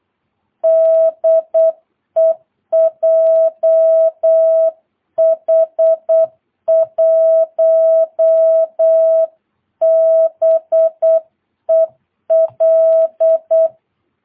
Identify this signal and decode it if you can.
Morse code, heard as a keyed tone, about 8 wpm, 650 Hz. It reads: DE JH1BEL
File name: jh8cbh / 2022cw